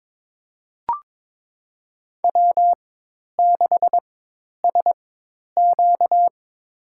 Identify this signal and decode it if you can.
Morse code, heard as a keyed tone, about 22 wpm, spaced as for 10 wpm, 700 Hz. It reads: W6SQ
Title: USA Call Signs - 4 Characters - 4x Character Spacing - 22wpm